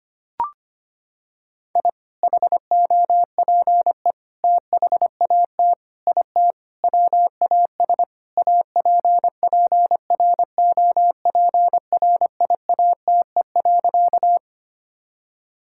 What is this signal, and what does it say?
Morse code, heard as a keyed tone, about 25 wpm, 700 Hz. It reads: I HOPE THAT IT WAS APPROPRIATE.
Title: Sentences from Top 1000 Words - Review Part 5 - 25wpm